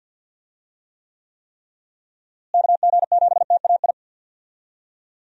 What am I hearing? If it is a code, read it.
Morse code, heard as a keyed tone, about 52 wpm, 700 Hz. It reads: KG7TAI
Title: USA Call Signs - 6 Characters - Repeated 3x - 50wpm